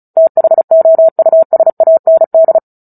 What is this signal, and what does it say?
Morse code, heard as a keyed tone, about 34 wpm, 650 Hz. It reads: THOUSAND
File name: Book Two / 09-Chapter 9 - 35 WPM